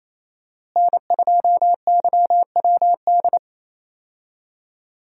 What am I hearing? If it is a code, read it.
Morse code, heard as a keyed tone, about 28 wpm, 700 Hz. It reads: N2YWD
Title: USA Call Signs - 5 Characters - Repeated 3x - 28wpm